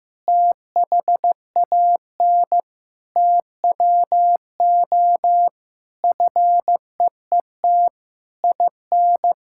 Morse code, heard as a keyed tone, about 15 wpm, 700 Hz. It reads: THAN TWO FEET IN
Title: Sentences from Top 1600 Words 15wpm